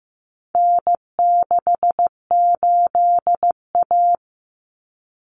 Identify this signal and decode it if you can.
Morse code, heard as a keyed tone, about 15 wpm, 700 Hz. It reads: N68A
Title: Common Cut Numbers - 4-Digit 15wpm